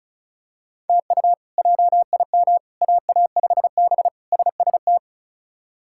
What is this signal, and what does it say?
Morse code, heard as a keyed tone, about 36 wpm, 700 Hz. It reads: TU JIM AA5B SST
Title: K1USN SST QSO Practice - 35wpm